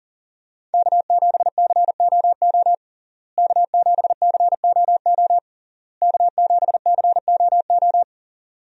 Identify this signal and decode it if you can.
Morse code, heard as a keyed tone, about 40 wpm, 700 Hz. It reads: K7COO K7COO K7COO
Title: USA Call Signs - 5 Characters - Repeated 3x - 40wpm